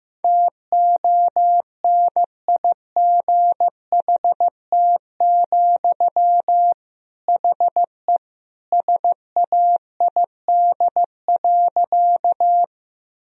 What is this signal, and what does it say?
Morse code, heard as a keyed tone, about 15 wpm, 700 Hz. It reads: TONIGHT, HE SAID.
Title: Sentences from Top 1000 Words - Review Part 3 - 15wpm